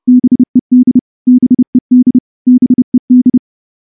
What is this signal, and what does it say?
Morse code, heard as a keyed tone, about 28 wpm, 250 Hz. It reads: BED BED BED
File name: morse-training-30WPM